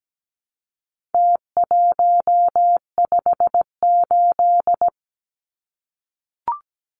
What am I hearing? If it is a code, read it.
Morse code, heard as a keyed tone, about 17 wpm, 700 Hz. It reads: T158
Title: Numbers - Contest Style - 4-Digit 17wpm